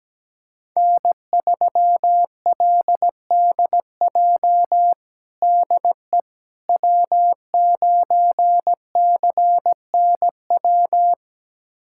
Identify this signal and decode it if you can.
Morse code, heard as a keyed tone, about 17 wpm, 700 Hz. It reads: N3LDJ DE W9CNW
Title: USA Callsign DE Callsign - 5 Characters - 17wpm